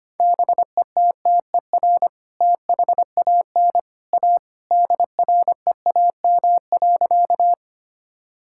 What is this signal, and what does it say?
Morse code, heard as a keyed tone, about 25 wpm, 700 Hz. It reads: BETTER THAN A DREAM.
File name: practice / 25wpm / Sentences with 2-Word Phrases - 25wpm